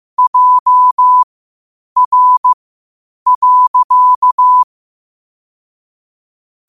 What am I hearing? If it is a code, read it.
Morse code, heard as a keyed tone, about 15 wpm, spaced as for 8 wpm, 1000 Hz. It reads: JR.
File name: qotdfiles / 8wpm